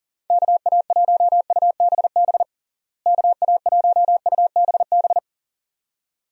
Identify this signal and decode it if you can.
Morse code, heard as a keyed tone, about 40 wpm, 700 Hz. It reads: KA1UBB KA1UBB
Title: USA Call Signs - 6 Characters - Repeated 2x - 40wpm